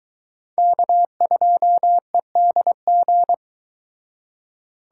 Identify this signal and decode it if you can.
Morse code, heard as a keyed tone, about 22 wpm, 700 Hz. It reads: K2EDG